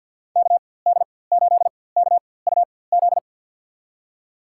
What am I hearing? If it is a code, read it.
Morse code, heard as a keyed tone, about 53 wpm, spaced as for 24 wpm, 700 Hz. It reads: KD8KUZ
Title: USA Call Signs - 6 Characters - 4x Character Spacing - 50wpm